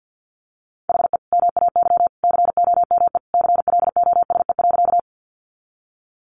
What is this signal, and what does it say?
Morse code, heard as a keyed tone, about 50 wpm, 700 Hz. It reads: HE MAY COME CLOSE.
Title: Sentences from Top 300 Words 50wpm